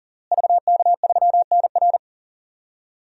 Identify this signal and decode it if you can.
Morse code, heard as a keyed tone, about 40 wpm, 700 Hz. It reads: VK3NR
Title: DXCC-100 Call Signs - 5 Characters - Repeated 3x - 40wpm